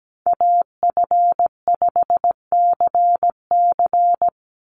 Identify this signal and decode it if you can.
Morse code, heard as a keyed tone, about 17 wpm, 700 Hz. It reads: AF5CC
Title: Call Signs Repeated 3x - 17wpm